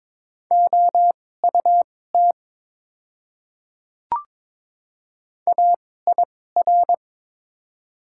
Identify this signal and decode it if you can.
Morse code, heard as a keyed tone, about 22 wpm, spaced as for 16 wpm, 700 Hz. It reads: OUT AIR
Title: 2x Character Spacing - Top 300 Words - 22wpm